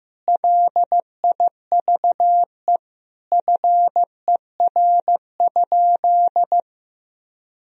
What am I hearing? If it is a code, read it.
Morse code, heard as a keyed tone, about 15 wpm, 700 Hz. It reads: LIVE FER?
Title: Sentences from Top 200 Words and Abbreviations 15wpm